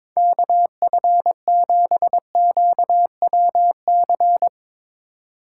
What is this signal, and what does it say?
Morse code, heard as a keyed tone, about 22 wpm, 700 Hz. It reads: KF7QWC
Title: USA Call Signs - 4x Character Spacing - 22wpm